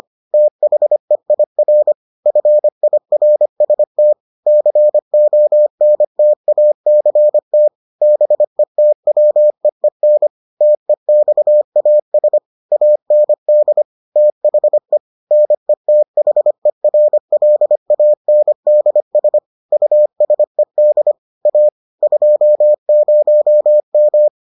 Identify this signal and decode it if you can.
Morse code, heard as a keyed tone, about 25 wpm, 600 Hz. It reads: THEIR FIRST CONTACT BETWEEN TEXAS AND THE NETHERLANDS USED A 20M